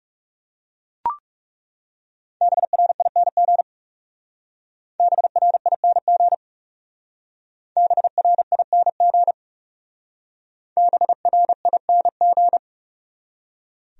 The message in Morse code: BRING BRING BRING BRING

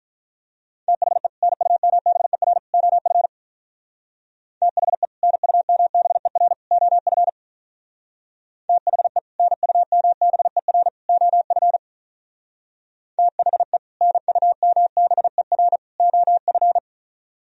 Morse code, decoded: THE NUMBER OF THE NUMBER OF THE NUMBER OF THE NUMBER OF

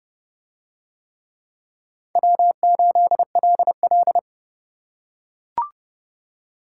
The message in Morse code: W8LL

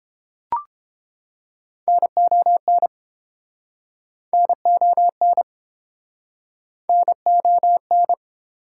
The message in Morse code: NON NON NON

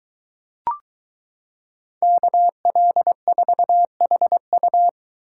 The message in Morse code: KL4HU

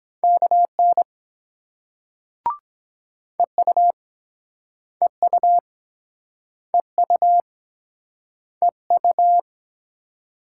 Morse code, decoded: KN EU EU EU EU

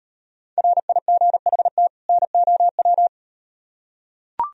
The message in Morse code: RIGHT NOW

